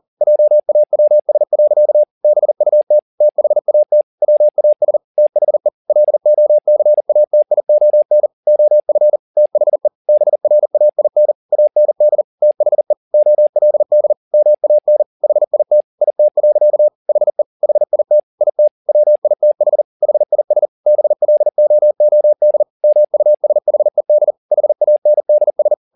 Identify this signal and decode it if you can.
Morse code, heard as a keyed tone, about 40 wpm, 600 Hz. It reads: JAWS. BUT THAT WAS THE LOCATION OF THE BRAIN AND THE OLD MAN HIT IT. HE HIT IT WITH HIS BLOOD MUSHED HANDS
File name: the old man and the sea 2 40wpm